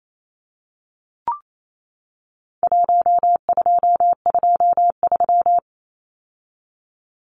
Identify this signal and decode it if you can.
Morse code, heard as a keyed tone, about 28 wpm, 700 Hz. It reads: 1223
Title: Numbers - Contest Style - 4-Digit 28wpm